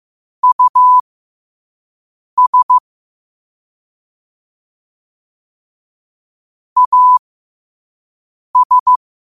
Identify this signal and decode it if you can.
Morse code, heard as a keyed tone, about 15 wpm, spaced as for 5 wpm, 1000 Hz. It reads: US AS
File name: qotdfiles / 5wpm